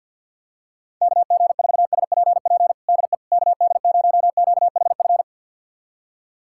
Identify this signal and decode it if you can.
Morse code, heard as a keyed tone, about 52 wpm, 690 Hz. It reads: KG4SPP DE KD0QHF
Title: USA Callsign DE Callsign - 6 Characters - 50wpm